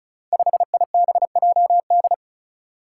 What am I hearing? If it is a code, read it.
Morse code, heard as a keyed tone, about 36 wpm, 700 Hz. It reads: 5IBJD
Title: Five Challenging Letter-Number Combinations 35wpm